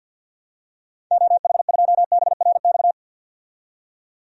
Morse code, heard as a keyed tone, about 51 wpm, 700 Hz. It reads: OH2BRX
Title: DXCC-100 Call Signs - 6 Characters - 50wpm